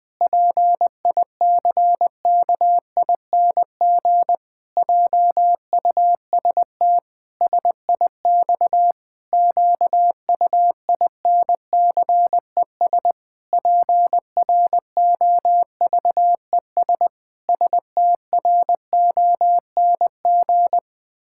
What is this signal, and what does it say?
Morse code, heard as a keyed tone, about 20 wpm, 700 Hz. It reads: PICKING JUST SIX QUINCES PROVES STRONG
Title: Pangram Challenge v2 - 20wpm